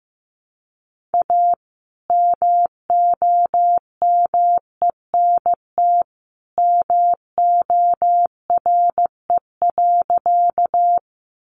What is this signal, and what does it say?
Morse code, heard as a keyed tone, about 15 wpm, 700 Hz. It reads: A MOMENT MORE.
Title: Sentences from Top 1000 Words 15wpm